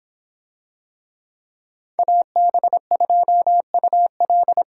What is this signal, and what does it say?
Morse code, heard as a keyed tone, about 24 wpm, 700 Hz. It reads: AB2UL